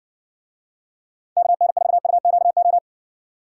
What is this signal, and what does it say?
Morse code, heard as a keyed tone, about 60 wpm, 700 Hz. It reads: KN4UQO